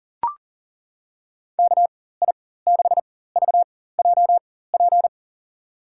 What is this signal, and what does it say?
Morse code, heard as a keyed tone, about 42 wpm, spaced as for 19 wpm, 690 Hz. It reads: KI6VJP
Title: USA Call Signs - 4x Character Spacing - 40wpm